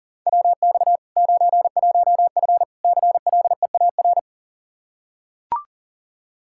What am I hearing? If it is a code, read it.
Morse code, heard as a keyed tone, about 40 wpm, 700 Hz. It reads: WX 91F CLEAR
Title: K1USN SST QSO Practice 40wpm